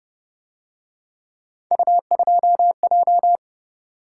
Morse code, heard as a keyed tone, about 30 wpm, 700 Hz. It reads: U2J